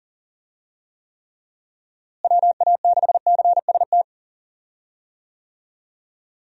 Morse code, heard as a keyed tone, about 40 wpm, 700 Hz. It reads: WA6CST
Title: USA Call Signs - 6 Characters - Repeated 3x - 40wpm